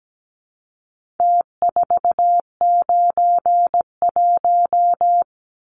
Morse code, heard as a keyed tone, about 17 wpm, 700 Hz. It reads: T491
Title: Numbers - Contest Style - 4-Digit 17wpm